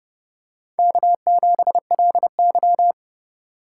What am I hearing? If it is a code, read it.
Morse code, heard as a keyed tone, about 30 wpm, 700 Hz. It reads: K7LY
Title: USA Call Signs - 4 Characters - 3x Character Spacing - 30wpm